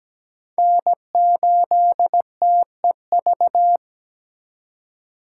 Morse code, heard as a keyed tone, about 17 wpm, 700 Hz. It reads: N8TEV
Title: USA Call Signs - 5 Characters - 2x Character Spacing - 17wpm